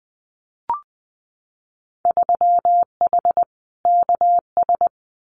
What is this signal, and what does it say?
Morse code, heard as a keyed tone, about 19 wpm, 700 Hz. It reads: 3H KS
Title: Winter Field Day Exchanges 20wpm